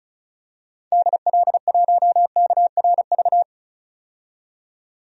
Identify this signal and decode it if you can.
Morse code, heard as a keyed tone, about 34 wpm, 700 Hz. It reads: DL1KRV
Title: DXCC-100 Call Signs - 6 Characters - 35wpm